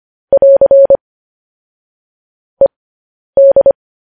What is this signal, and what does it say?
Morse code, heard as a keyed tone, about 25 wpm, spaced as for 10 wpm, 550 Hz. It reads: <AR> ED
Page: news headlines in morse code at 10 wpm